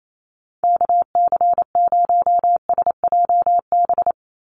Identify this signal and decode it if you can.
Morse code, heard as a keyed tone, about 28 wpm, 700 Hz. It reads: KC0SJB
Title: USA Call Signs 28wpm